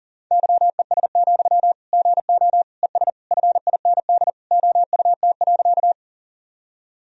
Morse code, heard as a keyed tone, about 40 wpm, 700 Hz. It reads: YES, GO ES FIND OUT.
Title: Sentences from Top 500 Words and Abbreviations 40wpm